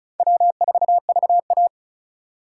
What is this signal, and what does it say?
Morse code, heard as a keyed tone, about 34 wpm, 700 Hz. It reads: W4VA
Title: USA Call Signs - 4 Characters - 6x Character Spacing - 35wpm